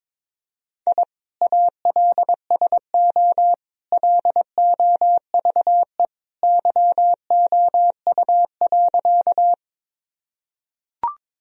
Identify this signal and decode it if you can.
Morse code, heard as a keyed tone, about 22 wpm, 700 Hz. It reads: I ALSO LOVE YOU.
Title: Sentences from Top 1000 Words - Review Part 1 - 22wpm